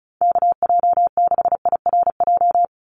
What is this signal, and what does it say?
Morse code, heard as a keyed tone, about 36 wpm, 700 Hz. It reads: KJ6IRJ
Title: USA Call Signs 35wpm